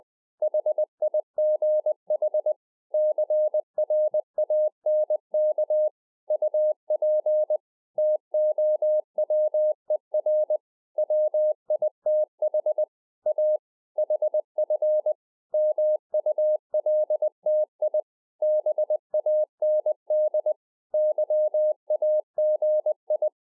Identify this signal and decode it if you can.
Morse code, heard as a keyed tone, about 20 wpm, 610 Hz. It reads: HIGH CRANK UP TOWER WITH A HF MULTI BAND YAGI